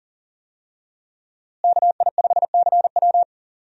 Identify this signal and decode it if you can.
Morse code, heard as a keyed tone, about 40 wpm, 700 Hz. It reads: KI5CW
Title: USA Call Signs - 5 Characters - Repeated 2x - 40wpm